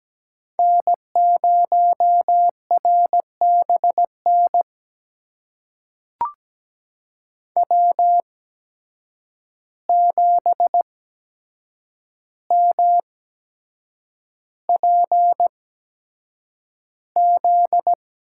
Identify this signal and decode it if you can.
Morse code, heard as a keyed tone, about 17 wpm, spaced as for 7 wpm, 700 Hz. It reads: N0RBN W7MPZ
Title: USA Call Signs - 5 Characters - 8x Character Spacing - 17wpm